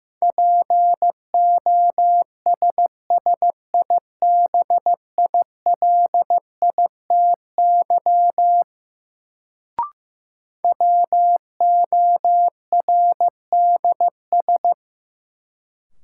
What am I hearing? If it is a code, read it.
Morse code, heard as a keyed tone, about 16 wpm, 700 Hz. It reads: POSSIBILITY WORDS
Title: Sets of 3 Words - New Word Drill - 15wpm